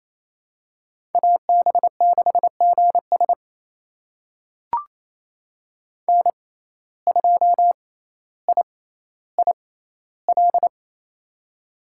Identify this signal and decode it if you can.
Morse code, heard as a keyed tone, about 28 wpm, spaced as for 14 wpm, 700 Hz. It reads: AB6GS N2IIL
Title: USA Call Signs - 5 Characters - 6x Character Spacing - 28wpm